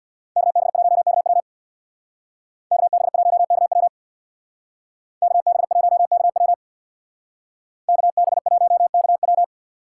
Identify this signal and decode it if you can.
Morse code, heard as a keyed tone, about 64 wpm, 700 Hz. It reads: KB1KW KB1KW KB1KW KB1KW